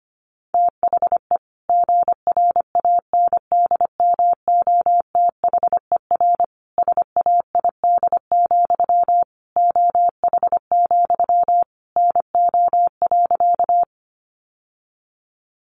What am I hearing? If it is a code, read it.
Morse code, heard as a keyed tone, about 25 wpm, 700 Hz. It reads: THE GRANDMOTHER SAID, OH, NO.
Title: Sentences from Top 1200 Words 25wpm